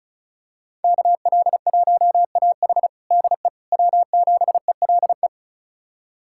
Morse code, heard as a keyed tone, about 34 wpm, 700 Hz. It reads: KL1AH DE W7ELE
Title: USA Callsign DE Callsign 35wpm